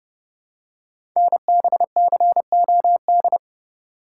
Morse code, heard as a keyed tone, about 30 wpm, 700 Hz. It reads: NBCOD
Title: Code Groups - 5 Letters - 30wpm